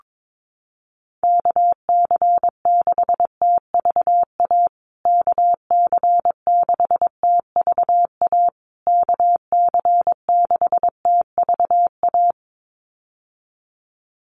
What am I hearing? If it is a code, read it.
Morse code, heard as a keyed tone, about 22 wpm, 700 Hz. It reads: KC6TVA KC6TVA KC6TVA
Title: USA Call Signs Repeated 3x 22wpm